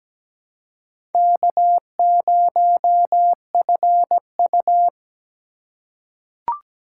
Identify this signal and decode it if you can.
Morse code, heard as a keyed tone, about 17 wpm, 700 Hz. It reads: K0FU
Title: USA Call Signs - 4x Character Spacing - 17wpm